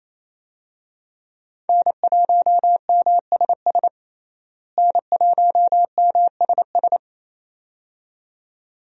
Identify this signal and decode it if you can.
Morse code, heard as a keyed tone, about 28 wpm, 700 Hz. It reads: N1MSS N1MSS
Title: USA Call Signs - 5 Characters - Repeated 2x - 28wpm